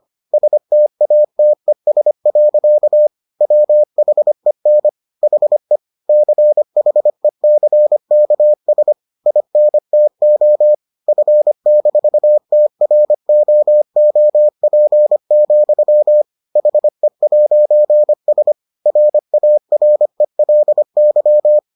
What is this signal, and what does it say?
Morse code, heard as a keyed tone, about 25 wpm, 600 Hz. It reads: STATES. WHEN HE CHECKS INTO F-TROOP, HE'S RARELY